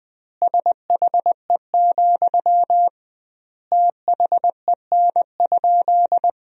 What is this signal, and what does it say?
Morse code, heard as a keyed tone, about 20 wpm, 700 Hz. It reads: SHE, THEN?
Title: 2x Word-Spacing - Sentences from Top 200 Words - 20wpm